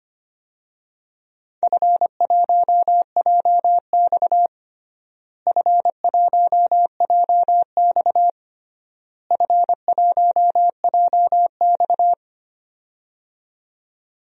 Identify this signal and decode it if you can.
Morse code, heard as a keyed tone, about 25 wpm, 700 Hz. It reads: F1JX F1JX F1JX
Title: DXCC-100 Call Signs - 4 Characters - Repeated 3x - 25wpm